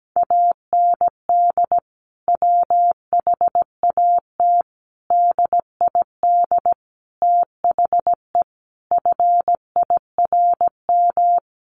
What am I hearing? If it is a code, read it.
Morse code, heard as a keyed tone, about 17 wpm, 700 Hz. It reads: AND WHAT DID THE FIRM
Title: Sentences from Top 1400 Words 17wpm